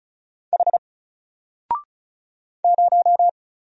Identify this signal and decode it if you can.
Morse code, heard as a keyed tone, about 34 wpm, 700 Hz. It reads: H 0
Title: Single Character (Letter:Number:Punctuation:Prosign) 35wpm